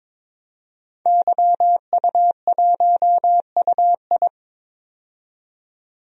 Morse code, heard as a keyed tone, about 22 wpm, 700 Hz. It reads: YU1UI
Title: DXCC-100 Call Signs - 5 Characters - 22wpm